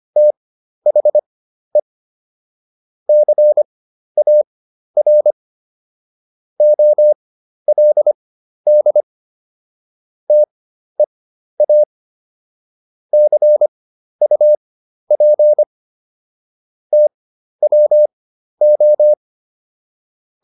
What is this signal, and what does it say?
Morse code, heard as a keyed tone, about 25 wpm, spaced as for 12 wpm, 600 Hz. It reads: THE CAR OLD TEA CUP TWO